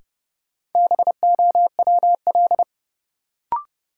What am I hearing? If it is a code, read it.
Morse code, heard as a keyed tone, about 32 wpm, 700 Hz. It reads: BOWL
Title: Sentences from Top 1200 Words - New Word Drill - 30wpm